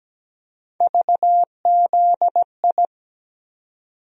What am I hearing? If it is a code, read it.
Morse code, heard as a keyed tone, about 17 wpm, 700 Hz. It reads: VZI
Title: USA Call Sign Postfixes - 3 Letters 17wpm